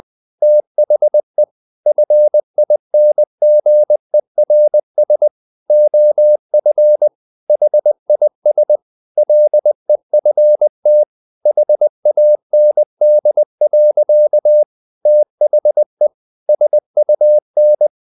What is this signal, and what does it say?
Morse code, heard as a keyed tone, about 20 wpm, 600 Hz. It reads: THE FINGERS OF HIS LEFT HAND. THE SUN
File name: the old man and the sea / the old man and the sea part 1 20wpm